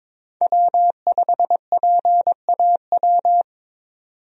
Morse code, heard as a keyed tone, about 22 wpm, 700 Hz. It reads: W5PAW
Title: USA Call Signs - 5 Characters - 22wpm